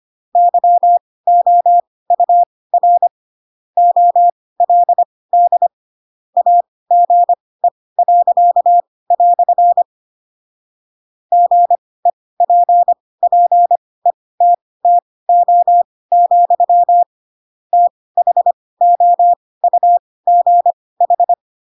Morse code, heard as a keyed tone, about 25 wpm, spaced as for 18 wpm, 700 Hz. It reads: YOUR OLD AGE." GEPPETTO, THOUGH